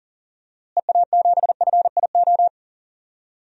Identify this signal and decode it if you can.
Morse code, heard as a keyed tone, about 37 wpm, 700 Hz. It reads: EA7FIO
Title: DXCC-100 Call Signs - 6 Characters - Repeated 2x - 40wpm